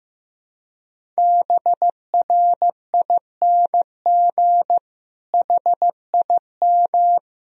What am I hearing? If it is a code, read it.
Morse code, heard as a keyed tone, about 15 wpm, 700 Hz. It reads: BRING HIM
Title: Sentences from Top 1000 Words - Review Part 3 - 15wpm